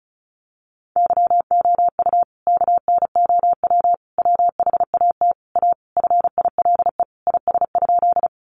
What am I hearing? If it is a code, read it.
Morse code, heard as a keyed tone, about 34 wpm, 700 Hz. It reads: YOU KNOW WHAT A FILE IS?
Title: Sentences from Top 1200 Words 35wpm